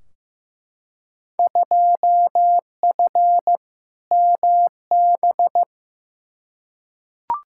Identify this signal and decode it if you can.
Morse code, heard as a keyed tone, about 15 wpm, 700 Hz. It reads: 2F MB
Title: Field Day Exchanges Encore - 15wpm